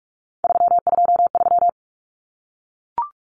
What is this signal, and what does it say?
Morse code, heard as a keyed tone, about 44 wpm, 700 Hz. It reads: 323
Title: Common Cut Numbers - 3-Digit 45wpm